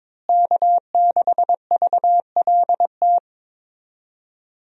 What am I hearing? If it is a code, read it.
Morse code, heard as a keyed tone, about 21 wpm, 700 Hz. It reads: K6VLT
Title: USA Call Signs - 5 Characters - Repeated 3x - 22wpm